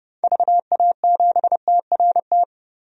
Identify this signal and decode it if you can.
Morse code, heard as a keyed tone, about 32 wpm, 700 Hz. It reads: VA7TRT
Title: DXCC-100 Call Signs - 6 Characters - 30wpm